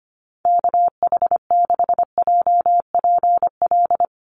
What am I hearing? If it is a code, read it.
Morse code, heard as a keyed tone, about 25 wpm, 700 Hz. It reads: KH6JPL
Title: USA Call Signs Repeated 3x 25wpm